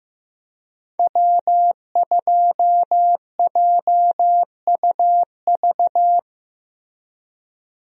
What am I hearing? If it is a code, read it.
Morse code, heard as a keyed tone, about 16 wpm, 700 Hz. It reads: W2JUV
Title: USA Call Signs - 5 Characters - Repeated 2x - 15wpm